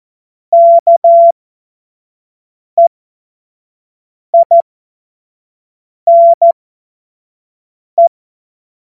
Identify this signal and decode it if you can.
Morse code, heard as a keyed tone, about 14 wpm, spaced as for 5 wpm, 680 Hz. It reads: KEINE
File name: Morsekurs / Witze / Witz 07. Dezember 2021, 25 F70